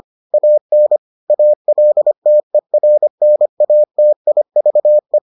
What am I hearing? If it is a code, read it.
Morse code, heard as a keyed tone, about 25 wpm, 600 Hz. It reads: AN ALTERNATIVE